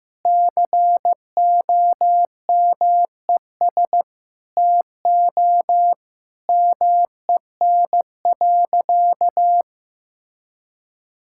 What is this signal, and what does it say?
Morse code, heard as a keyed tone, about 14 wpm, 700 Hz. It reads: COMES TO MEN.
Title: Sentences from Top 1000 Words - Review Part 6 - 15wpm